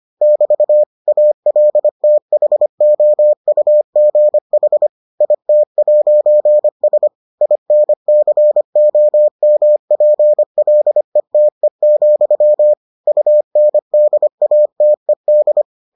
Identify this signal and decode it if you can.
Morse code, heard as a keyed tone, about 25 wpm, 600 Hz. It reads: <BT> ALTHOUGH IT'S INCOMPLETE, UNDATED